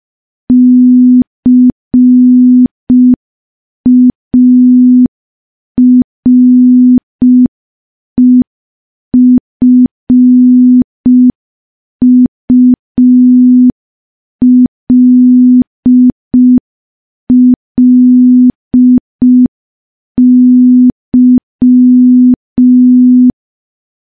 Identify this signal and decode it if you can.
Morse code, heard as a keyed tone, about 5 wpm, 250 Hz. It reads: CAREFULLY
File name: morse / morse-training-05WPM